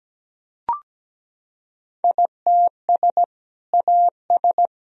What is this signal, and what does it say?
Morse code, heard as a keyed tone, about 17 wpm, 700 Hz. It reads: ITS AS
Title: Sentences from Top 1000 Words - Review Part 2 - 17wpm